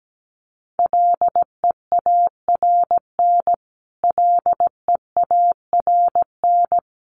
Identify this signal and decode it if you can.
Morse code, heard as a keyed tone, about 17 wpm, 700 Hz. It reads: LEARN LEARN
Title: Top 500 Words Repeated 17wpm